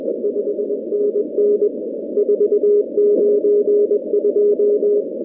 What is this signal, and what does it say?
Morse code, heard as a keyed tone, about 21 wpm, 410 Hz. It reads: 5NN 492
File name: contests / 2019-cw-wpx-kt5j-k5pi- -2330 0-41